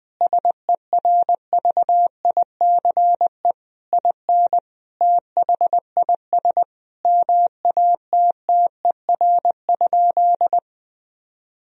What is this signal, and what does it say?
Morse code, heard as a keyed tone, about 21 wpm, 700 Hz. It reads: SERVICE IN THIS MATTER?
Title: Sentences from Top 1000 Words - Review Part 4 - 20wpm